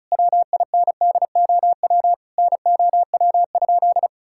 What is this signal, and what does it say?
Morse code, heard as a keyed tone, about 36 wpm, 700 Hz. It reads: WINDOW NOW?
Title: Sentences from Top 1600 Words 35wpm